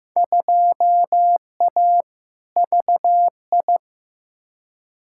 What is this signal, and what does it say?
Morse code, heard as a keyed tone, about 15 wpm, 700 Hz. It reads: 2A VI
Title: Field Day Exchanges Encore - 15wpm